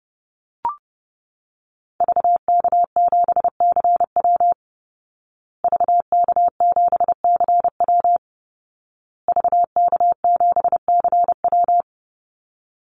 Morse code, heard as a keyed tone, about 30 wpm, 700 Hz. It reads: VK7CW VK7CW VK7CW
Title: Call Signs Repeated 2x - 30wpm